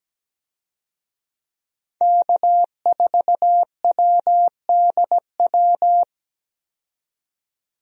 Morse code, heard as a keyed tone, about 17 wpm, 700 Hz. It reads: K4WDW
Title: USA Call Signs - 5 Characters - 17wpm